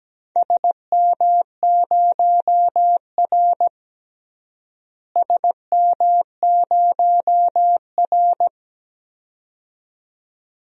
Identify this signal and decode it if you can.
Morse code, heard as a keyed tone, about 17 wpm, 700 Hz. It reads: SM0R SM0R
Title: DXCC-100 Call Signs - 4 Characters - Repeated 2x - 17wpm